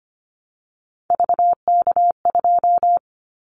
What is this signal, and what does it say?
Morse code, heard as a keyed tone, about 25 wpm, 700 Hz. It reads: VK2
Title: Callsign Prefixes for Top 50 DXCC - 25wpm